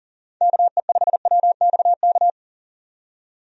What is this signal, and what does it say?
Morse code, heard as a keyed tone, about 40 wpm, 700 Hz. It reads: KE5WXK
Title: USA Call Signs - 6 Characters - 40wpm